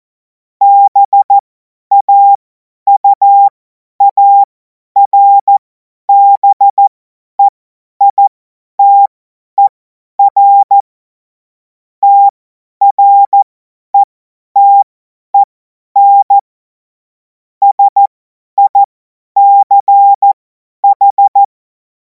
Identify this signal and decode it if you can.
Morse code, heard as a keyed tone, about 14 wpm, spaced as for 10 wpm, 800 Hz. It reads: BAUARBEITER TRETEN SICH
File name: Morsekurs / Witze / Witz 07. April 2026 50 F70